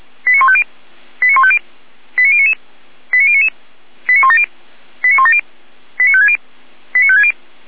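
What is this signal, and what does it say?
Five-tone selective call (ZVEI), heard as quick runs of five tones, tones of about 70 ms. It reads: 89150 89150 89099 89099 89170 89170 89560 89560
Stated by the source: ZVEI five-tone selective call received with an SDR.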